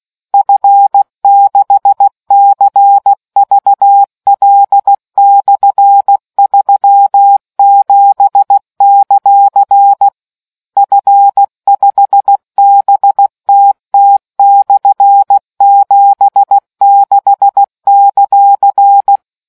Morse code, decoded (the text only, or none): F6CVL/37; F5BTT/76;